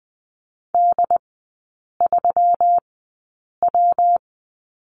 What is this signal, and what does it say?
Morse code, heard as a keyed tone, about 20 wpm, spaced as for 8 wpm, 700 Hz. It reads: D3W
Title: 133 Lesson 22e - Question Mark - Three Characters 20wpm